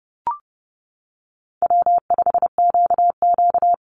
Morse code, heard as a keyed tone, about 30 wpm, 700 Hz. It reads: W5QQ
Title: Easy Call Signs Repeated 3x - 30wpm